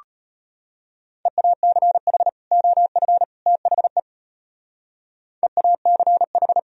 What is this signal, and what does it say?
Morse code, heard as a keyed tone, about 31 wpm, 700 Hz. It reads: EACH OF THE EACH